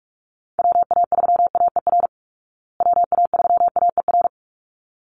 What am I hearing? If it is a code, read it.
Morse code, heard as a keyed tone, about 44 wpm, 700 Hz. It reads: WA3AER WA3AER
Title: Call Signs Repeated 3x - 45wpm